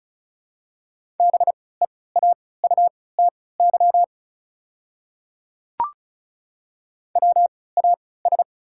Morse code, heard as a keyed tone, about 35 wpm, spaced as for 20 wpm, 690 Hz. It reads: BEAUTY WAS
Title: 3x Character Spacing - Top 500 Words - 35wpm